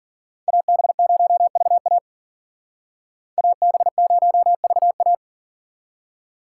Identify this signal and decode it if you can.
Morse code, heard as a keyed tone, about 52 wpm, 700 Hz. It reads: AB0VA AB0VA